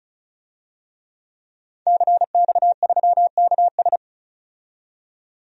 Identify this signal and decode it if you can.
Morse code, heard as a keyed tone, about 34 wpm, 700 Hz. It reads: CX3KS
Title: DXCC-100 Call Signs - 5 Characters - 35wpm